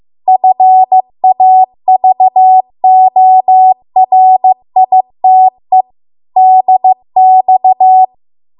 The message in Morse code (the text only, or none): FAVORITE DX